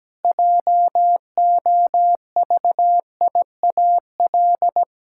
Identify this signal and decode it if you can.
Morse code, heard as a keyed tone, about 17 wpm, 700 Hz. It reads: JOVIAL
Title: Pangram Challenge v2 - 17wpm